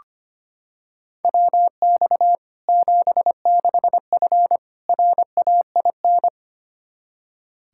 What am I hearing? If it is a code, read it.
Morse code, heard as a keyed tone, about 25 wpm, 700 Hz. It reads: WX 76F RAIN
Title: Practice QSO - Exchange 2 - Info - 25wpm